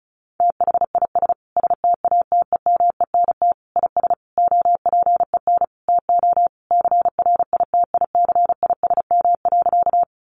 Morse code, heard as a keyed tone, about 34 wpm, 700 Hz. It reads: THIS STATEMENT IS OPEN TO CRITICISM.
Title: Sentences from Top 1300 Words 35wpm